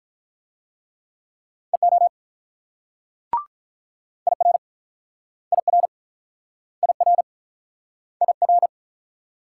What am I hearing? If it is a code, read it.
Morse code, heard as a keyed tone, about 46 wpm, 700 Hz. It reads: EO IR IR IR IR